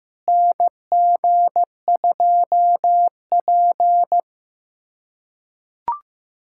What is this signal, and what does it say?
Morse code, heard as a keyed tone, about 16 wpm, 700 Hz. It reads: NG2P
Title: K1USN SST QSO Practice - 15wpm